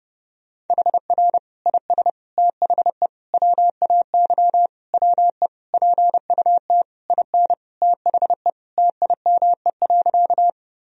Morse code, heard as a keyed tone, about 32 wpm, 700 Hz. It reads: HR IS THE WAY WE PUT IN THE TIME.
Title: Sentences from Top 200 Words and Abbreviations 30wpm